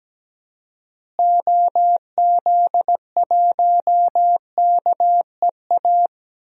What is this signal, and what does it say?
Morse code, heard as a keyed tone, about 17 wpm, 700 Hz. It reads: OZ1KEA
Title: DXCC-100 Call Signs - 6 Characters - Repeated 2x - 17wpm